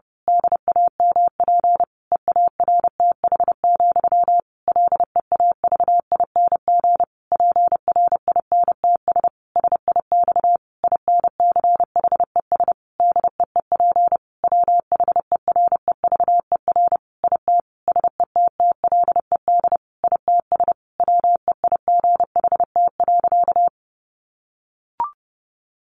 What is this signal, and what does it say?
Morse code, heard as a keyed tone, about 30 wpm, 700 Hz. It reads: DAMP EARTH, LEAVING PRINTS SIX INCHES DEEP WHEREVER IT SETTLED ITS WEIGHT.
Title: A Sound Of Thunder 30wpm